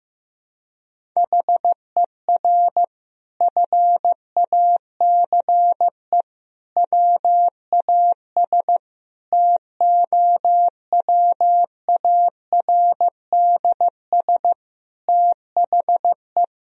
Morse code, heard as a keyed tone, about 15 wpm, 700 Hz. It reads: HER FACE WAS TOWARDS THE
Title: Sentences from Top 1000 Words - Review Part 2 - 15wpm